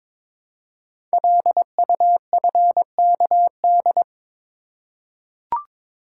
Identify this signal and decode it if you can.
Morse code, heard as a keyed tone, about 23 wpm, 700 Hz. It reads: LUFKD
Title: Code Groups - 5 Letters - 22wpm